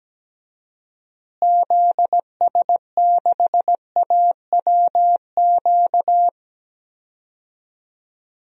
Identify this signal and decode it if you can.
Morse code, heard as a keyed tone, about 17 wpm, 700 Hz. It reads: ZS6AWQ
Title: DXCC-100 Call Signs - 6 Characters - Repeated 3x - 17wpm